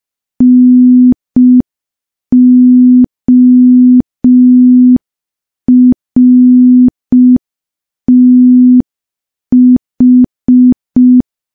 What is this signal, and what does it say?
Morse code, heard as a keyed tone, about 5 wpm, 250 Hz. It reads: NORTH
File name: morse / morse-training-05WPM